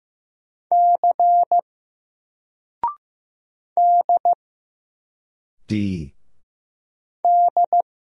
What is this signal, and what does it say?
Morse code, heard as a keyed tone, about 15 wpm, 700 Hz. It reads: C D D
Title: Single Letter-Number 15wpm